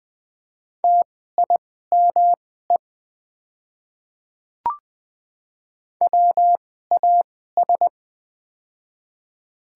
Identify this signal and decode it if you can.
Morse code, heard as a keyed tone, about 20 wpm, spaced as for 14 wpm, 700 Hz. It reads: TIME WAS
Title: 2x Character Spacing - Top 200 Words - 20wpm